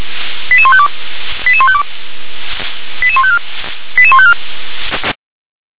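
Five-tone selective call (ZVEI), heard as quick runs of five tones, tones of about 70 ms. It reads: 88152 88152 88154 88154
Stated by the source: Received on an SDR, ZVEI five-tone code 88107.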